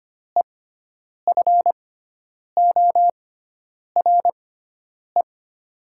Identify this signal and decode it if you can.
Morse code, heard as a keyed tone, about 25 wpm, spaced as for 9 wpm, 700 Hz. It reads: EFORE
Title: 6x Character Spacing - Top 300 Words - 25wpm